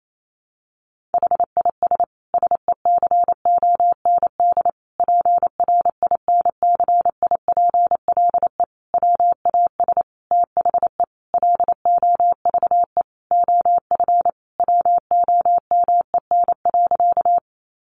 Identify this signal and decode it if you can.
Morse code, heard as a keyed tone, about 28 wpm, 700 Hz. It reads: HIS SECOND PRINCIPLE WAS THE LOVE OF WOMEN.
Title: Sentences from Top 1400 Words 28wpm